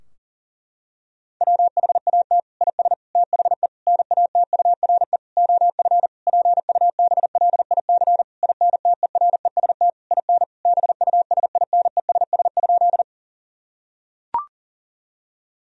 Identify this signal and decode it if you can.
Morse code, heard as a keyed tone, about 40 wpm, 700 Hz. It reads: WHAT IS THE NATURE OF PUBLIC INTEREST IN BUSINESS?
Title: Sentences from Top 1000 Words - Review Part 3 - 40wpm